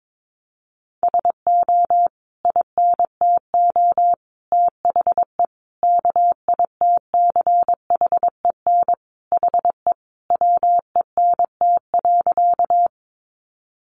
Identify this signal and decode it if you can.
Morse code, heard as a keyed tone, about 22 wpm, 700 Hz. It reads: SO INTO THE KITCHEN HE WENT.
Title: Sentences from Top 1400 Words 22wpm